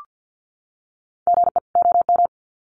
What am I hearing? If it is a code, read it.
Morse code, heard as a keyed tone, about 50 wpm, 700 Hz. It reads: GE OM
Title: Common QSO Phrases 50wpm